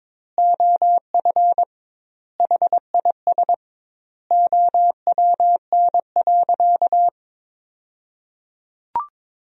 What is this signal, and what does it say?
Morse code, heard as a keyed tone, about 22 wpm, 700 Hz. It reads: OF HIS OWN.
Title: 2x Word-Spacing - Sentences from Top 300 Words - 22wpm